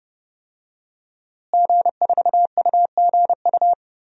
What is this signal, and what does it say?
Morse code, heard as a keyed tone, about 31 wpm, 700 Hz. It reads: G4UGU
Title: DXCC-100 Call Signs - 5 Characters - Repeated 2x - 30wpm